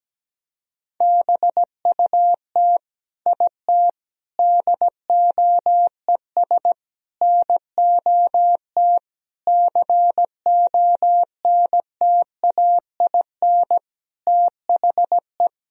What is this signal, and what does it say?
Morse code, heard as a keyed tone, about 17 wpm, 700 Hz. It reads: BUT IT DOES NOT CONTAIN THE
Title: Sentences from Top 1000 Words - Review Part 2 - 17wpm